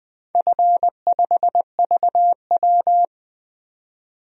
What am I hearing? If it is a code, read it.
Morse code, heard as a keyed tone, about 20 wpm, 700 Hz. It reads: F5VW
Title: DXCC-100 Call Signs - 4 Characters - 20wpm